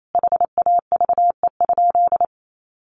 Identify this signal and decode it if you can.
Morse code, heard as a keyed tone, about 28 wpm, 700 Hz. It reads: HAVE?
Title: Sentences from Top 1000 Words 28wpm